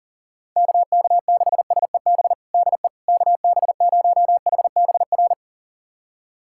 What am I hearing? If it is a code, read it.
Morse code, heard as a keyed tone, about 40 wpm, 700 Hz. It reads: KK6SEB DE KB0HBR
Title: USA Callsign DE Callsign - 6 Characters - 40wpm